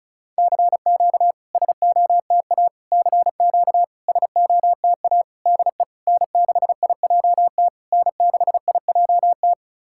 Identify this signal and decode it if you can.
Morse code, heard as a keyed tone, about 36 wpm, 700 Hz. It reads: CQ SOTA CQ SOTA DE N6IJT N6IJT
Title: SOTA QSO Practice 35wpm